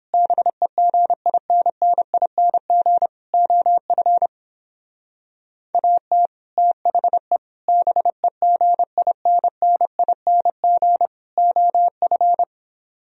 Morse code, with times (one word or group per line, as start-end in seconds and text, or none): 0.1-3.1 BEGINNING
3.3-4.3 OF
5.7-6.3 AT
6.6-7.4 THE
7.7-11.1 BEGINNING
11.4-12.4 OF